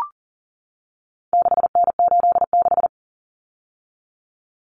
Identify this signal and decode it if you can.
Morse code, heard as a keyed tone, about 40 wpm, 700 Hz. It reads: 6N86
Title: Common Cut Numbers - 4-Digit 40wpm